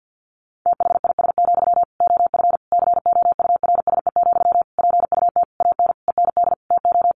Morse code, heard as a keyed tone, about 50 wpm, 700 Hz. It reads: THIS, OF COURSE, PUT AN END TO
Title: Sentences from Top 1400 Words 50wpm